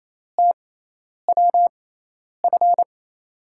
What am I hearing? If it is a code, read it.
Morse code, heard as a keyed tone, about 28 wpm, spaced as for 10 wpm, 700 Hz. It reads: TWF
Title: USA Call Signs - 6 Characters - 6x Character Spacing - 28wpm